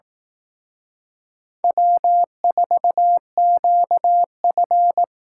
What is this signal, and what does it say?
Morse code, heard as a keyed tone, about 19 wpm, 700 Hz. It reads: W4QF
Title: Speed Racing - USA 1x2 Call Signs 15wpm